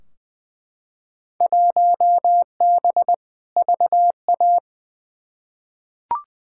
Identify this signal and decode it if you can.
Morse code, heard as a keyed tone, about 20 wpm, 700 Hz. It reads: 1B VA
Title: Field Day Exchanges - 20wpm